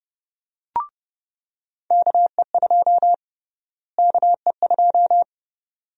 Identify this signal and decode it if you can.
Morse code, heard as a keyed tone, about 30 wpm, 700 Hz. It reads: KE2 KE2
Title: USA Call Sign Prefixes - Repeated 2x - 30wpm